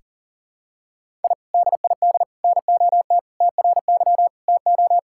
Morse code, heard as a keyed tone, about 40 wpm, 700 Hz. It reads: I DID NOT TRY TO
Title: Sentences from Top 1000 Words - Review Part 5 - 40wpm